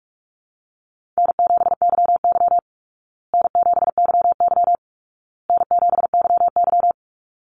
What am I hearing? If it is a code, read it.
Morse code, heard as a keyed tone, about 47 wpm, 700 Hz. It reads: N7YY N7YY N7YY
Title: Call Signs Repeated 2x - 45wpm